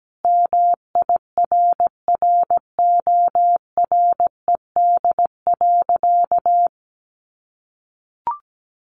Morse code, heard as a keyed tone, about 17 wpm, 700 Hz. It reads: MIRRORED.
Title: A Sound Of Thunder 17wpm